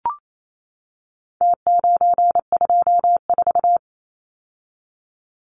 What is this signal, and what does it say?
Morse code, heard as a keyed tone, about 28 wpm, 700 Hz. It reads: T924
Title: Numbers - Contest Style - 4-Digit 28wpm